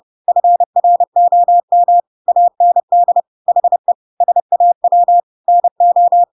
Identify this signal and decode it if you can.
Morse code, heard as a keyed tone, about 30 wpm, 700 Hz. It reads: FROM AND HE SAW NO
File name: Pinocchio / Pinocchio ch1 30 30 ik1hge